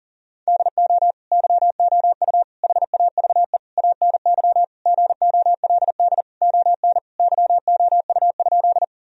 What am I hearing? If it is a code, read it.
Morse code, heard as a keyed tone, about 40 wpm, 700 Hz. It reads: DO YOU HAVE ANY GOLD ON YOU?
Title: Sentences from Top 1000 Words - Review Part 3 - 40wpm